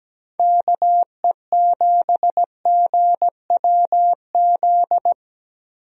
Morse code, heard as a keyed tone, about 17 wpm, 700 Hz. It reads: KE7GWZ
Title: USA Call Signs - 6 Characters - Repeated 3x - 17wpm